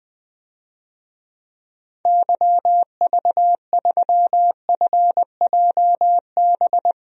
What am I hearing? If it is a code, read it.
Morse code, heard as a keyed tone, about 20 wpm, 700 Hz. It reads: YV3FJB
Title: DXCC-100 Call Signs - 6 Characters - 20wpm